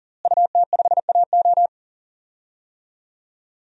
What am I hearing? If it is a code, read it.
Morse code, heard as a keyed tone, about 40 wpm, 700 Hz. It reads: UT5AO
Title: DXCC-100 Call Signs - 5 Characters - Repeated 3x - 40wpm